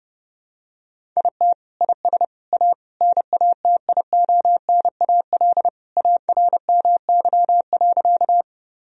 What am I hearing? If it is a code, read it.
Morse code, heard as a keyed tone, about 32 wpm, 700 Hz. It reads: IT IS A NATIONAL ARMY.
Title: Sentences from Top 1000 Words - Review Part 4 - 30wpm